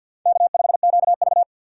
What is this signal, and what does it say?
Morse code, heard as a keyed tone, about 48 wpm, 700 Hz. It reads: K5QV
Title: USA Call Signs - 4 Characters - 3x Character Spacing - 50wpm